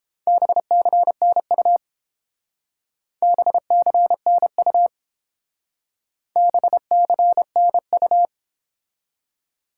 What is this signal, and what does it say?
Morse code, heard as a keyed tone, about 30 wpm, 700 Hz. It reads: BCNU BCNU BCNU